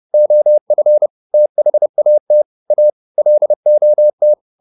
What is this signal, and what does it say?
Morse code, heard as a keyed tone, about 30 wpm, 600 Hz. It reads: OF THAT A LOT